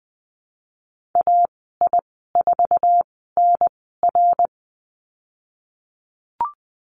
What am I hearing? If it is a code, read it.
Morse code, heard as a keyed tone, about 20 wpm, spaced as for 15 wpm, 700 Hz. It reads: AI4NR
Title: 055 Lesson 10g - Letter r - Call Signs 20wpm